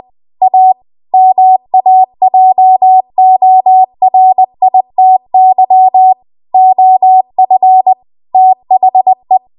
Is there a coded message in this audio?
Yes: Morse code from 0.4 to 9.4 s, heard as a keyed tone, about 20 wpm, 750 Hz. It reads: A MAJORITY OF THE